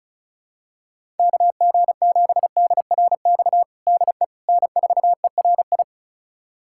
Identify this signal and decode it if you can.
Morse code, heard as a keyed tone, about 34 wpm, 700 Hz. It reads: KG7DRX DE N4ERI
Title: USA Callsign DE Callsign 35wpm